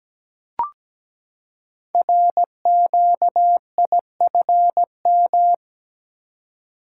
Morse code, heard as a keyed tone, about 17 wpm, 700 Hz. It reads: RQIFM
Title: Code Groups - 5 Letters - 17wpm